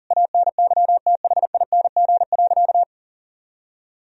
Morse code, heard as a keyed tone, about 37 wpm, 700 Hz. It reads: ANYTHING.
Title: Sentences from Top 1000 Words - Review Part 3 - 40wpm